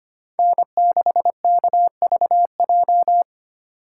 Morse code, heard as a keyed tone, about 25 wpm, 700 Hz. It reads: N6KVJ
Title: USA Call Signs - 3x Character Spacing - 25wpm